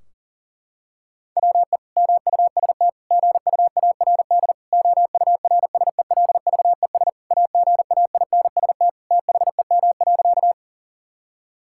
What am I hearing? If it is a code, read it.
Morse code, heard as a keyed tone, about 37 wpm, 700 Hz. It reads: WE MUST GUARD OURSELVES AGAINST THEM.
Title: Sentences from Top 1600 Words 40wpm